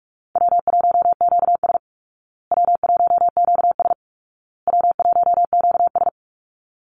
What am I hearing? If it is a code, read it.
Morse code, heard as a keyed tone, about 44 wpm, 700 Hz. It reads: W1QS W1QS W1QS
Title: Easy Call Signs Repeated 3x - 45wpm